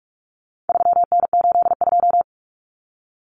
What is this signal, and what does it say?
Morse code, heard as a keyed tone, about 44 wpm, 700 Hz. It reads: 3N82
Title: Common Cut Numbers - 4-Digit 45wpm